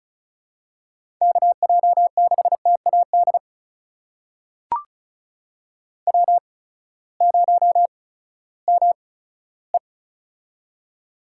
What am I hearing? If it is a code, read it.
Morse code, heard as a keyed tone, about 36 wpm, spaced as for 18 wpm, 700 Hz. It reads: KJ6TAD W0ME
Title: USA Call Signs - 8x Character Spacing - 35wpm